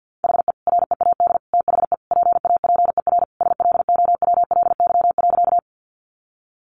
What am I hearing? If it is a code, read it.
Morse code, heard as a keyed tone, about 50 wpm, 700 Hz. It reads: HE READ THE PAPER SLOWLY.
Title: Sentences from Top 1100 Words 50wpm